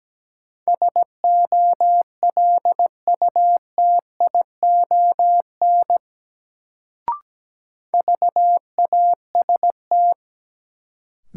Morse code, heard as a keyed tone, about 17 wpm, 700 Hz. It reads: SOLUTION VAST
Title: Sentences from Top 1300 Words - New Word Drill - 17wpm